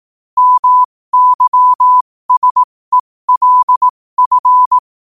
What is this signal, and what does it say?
Morse code, heard as a keyed tone, about 18 wpm, spaced as for 15 wpm, 1000 Hz. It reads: MYSELF
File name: qotdfiles / 15wpm